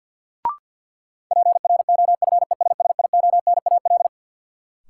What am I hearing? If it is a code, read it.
Morse code, heard as a keyed tone, about 50 wpm, 700 Hz. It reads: PROFESSIONAL
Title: Sentences from Top 1000 Words - Review Part 4 - New Word Drill - 50wpm